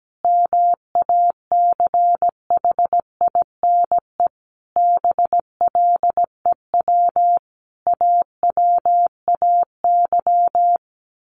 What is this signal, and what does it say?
Morse code, heard as a keyed tone, about 17 wpm, 700 Hz. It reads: MACHINE BLEW AWAY
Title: A Sound Of Thunder 17wpm